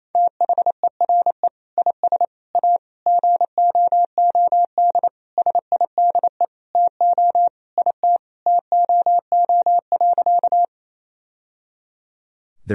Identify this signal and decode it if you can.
Morse code, heard as a keyed tone, about 27 wpm, 700 Hz. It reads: THERE IS A GOOD SIDE TO IT TOO.
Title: Sentences from Top 1000 Words - Review Part 1 - 28wpm